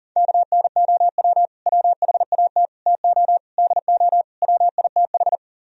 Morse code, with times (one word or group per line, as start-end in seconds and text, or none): 0.2-1.5 KNOW
1.7-2.7 WHAT
2.9-3.4 TO
3.6-4.2 DO
4.4-5.4 WITH